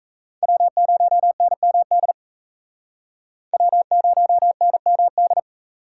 Morse code, heard as a keyed tone, about 35 wpm, 700 Hz. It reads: W0NMD W0NMD
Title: Speed Racing - USA 1x3 Call Signs 28wpm